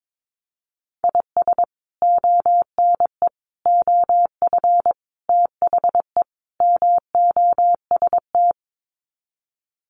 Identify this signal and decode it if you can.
Morse code, heard as a keyed tone, about 22 wpm, 700 Hz. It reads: IS ONE OF THE MOST
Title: Sets of 5 Words 22wpm